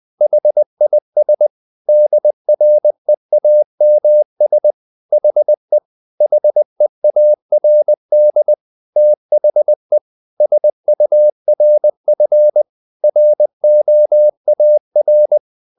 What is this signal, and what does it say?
Morse code, heard as a keyed tone, about 20 wpm, 600 Hz. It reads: HIS DREAMS HE HEARD THE SURF ROAR
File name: the old man and the sea / the old man and the sea part 1 20wpm